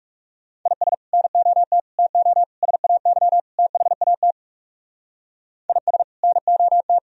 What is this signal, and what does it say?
Morse code, heard as a keyed tone, about 37 wpm, 700 Hz. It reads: IS NOT TO SAY THAT IS NOT
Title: Speed-Racing - Sets of 5 Words 30wpm